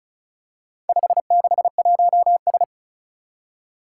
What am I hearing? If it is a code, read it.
Morse code, heard as a keyed tone, about 36 wpm, 690 Hz. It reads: 561S
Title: Four Challenging Letter-Number Combinations 35wpm